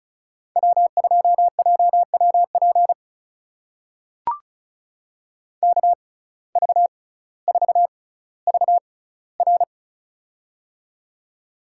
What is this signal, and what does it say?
Morse code, heard as a keyed tone, about 34 wpm, spaced as for 18 wpm, 700 Hz. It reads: W2JWP KV4VR
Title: USA Call Signs - 5 Characters - 6x Character Spacing - 35wpm